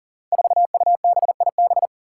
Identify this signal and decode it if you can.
Morse code, heard as a keyed tone, about 40 wpm, 700 Hz. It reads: 4UBIB
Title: Five Challenging Letter-Number Combinations 40wpm